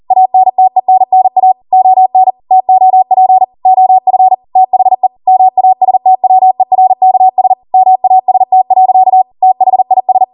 Morse code, heard as a keyed tone, about 40 wpm, 750 Hz. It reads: ANTENNA ON TOP OF THE MASTWERKS MAST. THIS
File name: Morse / 40WPM